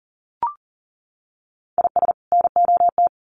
Morse code, heard as a keyed tone, about 40 wpm, 700 Hz. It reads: IS NOT
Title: Sets of 2 Words 40wpm